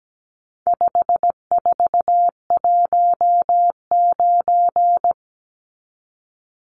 Morse code, heard as a keyed tone, about 17 wpm, 700 Hz. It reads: 5419
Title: Numbers - 4-Digit 17wpm